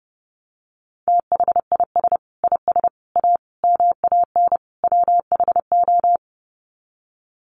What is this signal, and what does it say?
Morse code, heard as a keyed tone, about 30 wpm, 700 Hz. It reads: THIS IS A MAN WHO
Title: Sets of 5 Words Encore2 30wpm